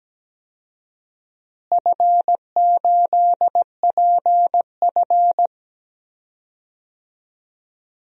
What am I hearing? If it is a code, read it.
Morse code, heard as a keyed tone, about 17 wpm, 700 Hz. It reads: F8PF
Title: DXCC-100 Call Signs - 4 Characters - Repeated 2x - 17wpm